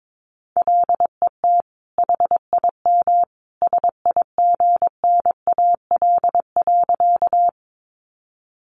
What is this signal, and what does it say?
Morse code, heard as a keyed tone, about 22 wpm, 700 Hz. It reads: LET HIM SIGNAL.
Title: Sentences from Top 1400 Words 22wpm